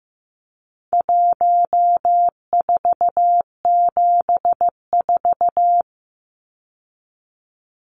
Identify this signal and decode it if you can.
Morse code, heard as a keyed tone, about 15 wpm, 700 Hz. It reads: 1474
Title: Numbers - 4-Digit 15wpm